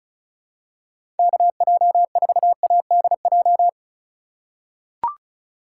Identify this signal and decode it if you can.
Morse code, heard as a keyed tone, about 34 wpm, 700 Hz. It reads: KJ4ADJ
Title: USA Call Signs - 6 Characters - 4x Character Spacing - 35wpm